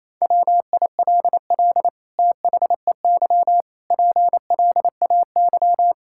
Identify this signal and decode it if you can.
Morse code, heard as a keyed tone, about 28 wpm, 700 Hz. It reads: WILL THEY PLAY
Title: Sentences from Top 1000 Words - Review Part 6 - 28wpm